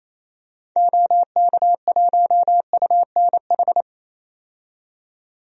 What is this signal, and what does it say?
Morse code, heard as a keyed tone, about 28 wpm, 700 Hz. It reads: OK1UNH
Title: DXCC-100 Call Signs - 6 Characters - 28wpm